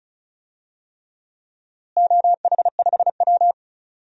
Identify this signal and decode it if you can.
Morse code, heard as a keyed tone, about 36 wpm, 700 Hz. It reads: OH5W
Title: DXCC-100 Call Signs - 4 Characters - Repeated 3x - 35wpm